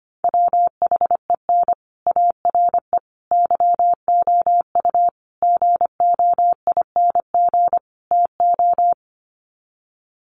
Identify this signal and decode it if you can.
Morse code, heard as a keyed tone, about 25 wpm, 700 Hz. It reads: WHEN ARE YOU GOING TO
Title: Sets of 5 Words Encore2 25wpm